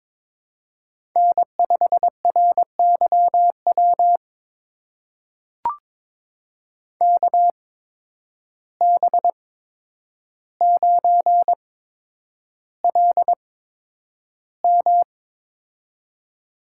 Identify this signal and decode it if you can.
Morse code, heard as a keyed tone, about 22 wpm, spaced as for 10 wpm, 700 Hz. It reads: N5RYW KB9LM
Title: USA Call Signs - 5 Characters - 8x Character Spacing - 22wpm